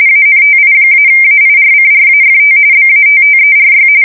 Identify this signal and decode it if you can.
RTTY, about 45.45 baud. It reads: ANTI-AIRCRAFT BATTERY
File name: fsk170 / USS-Massachusetts